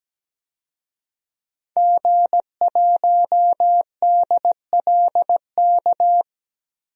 Morse code, heard as a keyed tone, about 17 wpm, 700 Hz. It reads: G1DLK
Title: DXCC-100 Call Signs - 5 Characters - Repeated 2x - 17wpm